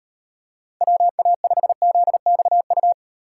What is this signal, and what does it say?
Morse code, heard as a keyed tone, about 36 wpm, 700 Hz. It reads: WA5ZXU